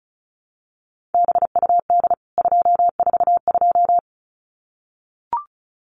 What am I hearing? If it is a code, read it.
Morse code, heard as a keyed tone, about 34 wpm, 700 Hz. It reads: BUD 242
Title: CWT Exchange 35wpm